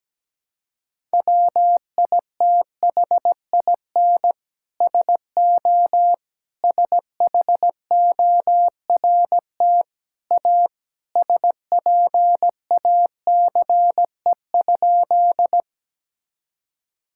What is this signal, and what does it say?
Morse code, heard as a keyed tone, about 17 wpm, 700 Hz. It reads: WITHIN SO SHORT A SPACE?
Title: Sentences from Top 1000 Words - Review Part 5 - 17wpm